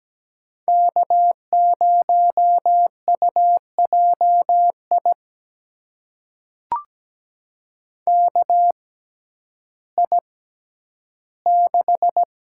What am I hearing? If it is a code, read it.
Morse code, heard as a keyed tone, about 17 wpm, spaced as for 10 wpm, 700 Hz. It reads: K0UJI KI6
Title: USA Call Signs - 6x Character Spacing - 17wpm